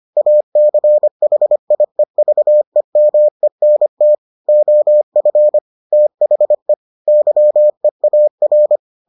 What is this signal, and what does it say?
Morse code, heard as a keyed tone, about 25 wpm, 600 Hz. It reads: ACHIEVEMENT OF THE YEAR